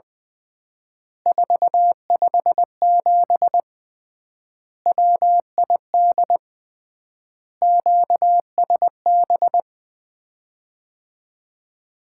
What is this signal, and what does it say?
Morse code, heard as a keyed tone, about 20 wpm, 700 Hz. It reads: 457 WID QSB
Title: Common QSO Phrases - 3x Word-Spacing - 20wpm